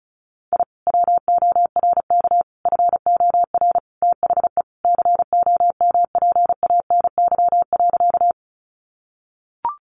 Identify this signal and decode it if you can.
Morse code, heard as a keyed tone, about 34 wpm, 700 Hz. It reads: I WORK FOR THE COMPANY.
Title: Sentences from Top 600 Words 35wpm